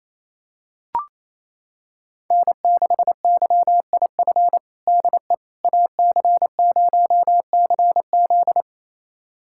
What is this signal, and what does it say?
Morse code, heard as a keyed tone, about 27 wpm, 700 Hz. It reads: N6YIF DE AC0CZ
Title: USA Callsign DE Callsign - 5 Characters - 28wpm